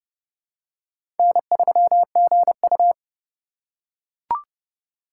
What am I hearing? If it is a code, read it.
Morse code, heard as a keyed tone, about 30 wpm, 700 Hz. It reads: N3GU
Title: USA Call Signs - 4 Characters - 4x Character Spacing - 30wpm